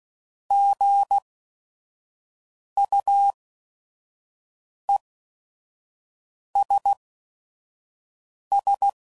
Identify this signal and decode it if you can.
Morse code, heard as a keyed tone, about 16 wpm, spaced as for 5 wpm, 800 Hz. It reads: GUESS
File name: Prefixes / By First Letter / Gs